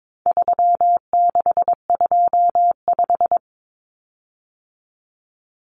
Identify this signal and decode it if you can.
Morse code, heard as a keyed tone, about 22 wpm, 700 Hz. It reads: 3625
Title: Common Cut Numbers - 4-Digit 22wpm